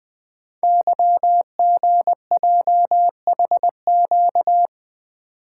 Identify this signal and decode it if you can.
Morse code, heard as a keyed tone, about 20 wpm, 700 Hz. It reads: YGJHQ
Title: Code Groups - 5 Letters - 20wpm